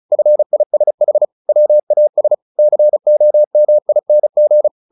Morse code, heard as a keyed tone, about 34 wpm, 600 Hz. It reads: FISH WAS COMING
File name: the old man and the sea 2 35wpm